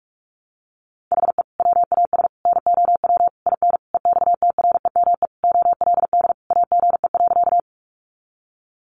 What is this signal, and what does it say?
Morse code, heard as a keyed tone, about 44 wpm, 700 Hz. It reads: HE WAS NOW IN EXTREME OLD AGE.
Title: Sentences from Top 1300 Words 45wpm